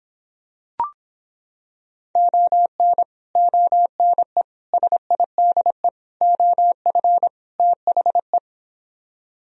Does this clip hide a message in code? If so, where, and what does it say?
Yes: Morse code from 2.2 to 8.4 s, heard as a keyed tone, about 26 wpm, 700 Hz. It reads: ON ONE SIDE OF THE